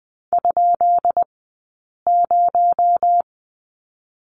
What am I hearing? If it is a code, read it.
Morse code, heard as a keyed tone, about 20 wpm, spaced as for 8 wpm, 700 Hz. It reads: ?0
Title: 217 Lesson 35d - Number 0 - Two Characters 20wpm